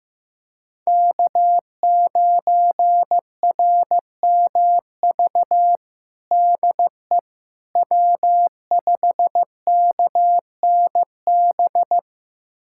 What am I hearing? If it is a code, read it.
Morse code, heard as a keyed tone, about 15 wpm, 700 Hz. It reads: K9RMV DE W5KNB
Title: USA Callsign DE Callsign - 5 Characters - 15wpm